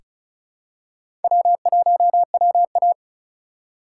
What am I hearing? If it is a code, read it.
Morse code, heard as a keyed tone, about 34 wpm, 700 Hz. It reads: W1WA
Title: USA Call Signs - 4 Characters - 2x Character Spacing - 35wpm